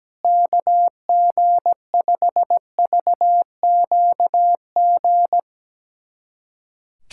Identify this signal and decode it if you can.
Morse code, heard as a keyed tone, about 17 wpm, 700 Hz. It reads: KG5VQG
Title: USA Callsign DE Callsign 17wpm